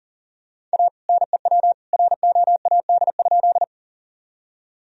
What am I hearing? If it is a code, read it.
Morse code, heard as a keyed tone, about 40 wpm, 700 Hz. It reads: A NEW ROAD?
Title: Sentences from Top 1000 Words - Review Part 2 - 40wpm